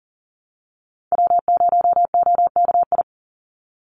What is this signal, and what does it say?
Morse code, heard as a keyed tone, about 40 wpm, 700 Hz. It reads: W0OKI
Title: USA Call Signs 40wpm